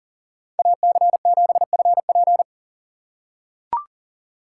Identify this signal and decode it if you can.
Morse code, heard as a keyed tone, about 37 wpm, 700 Hz. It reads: AC7FP
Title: USA Call Signs - 5 Characters - 4x Character Spacing - 40wpm